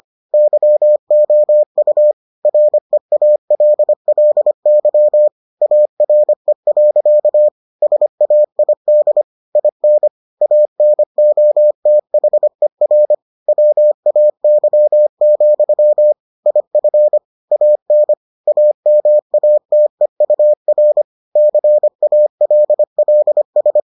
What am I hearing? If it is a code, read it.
Morse code, heard as a keyed tone, about 25 wpm, 600 Hz. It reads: YOU REALLY ARE. SAID IN ANOTHER WAY, IF AN AMATEUR CALLS